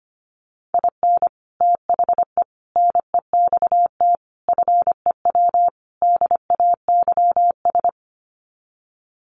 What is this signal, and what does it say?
Morse code, heard as a keyed tone, about 25 wpm, 700 Hz. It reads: IN THE NEXT FEW DAYS
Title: Sets of 5 Words Encore2 25wpm